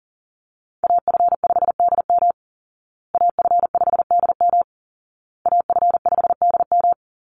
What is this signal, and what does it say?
Morse code, heard as a keyed tone, about 40 wpm, 700 Hz. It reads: AF5DM AF5DM AF5DM
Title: Call Signs Repeated 2x - 40wpm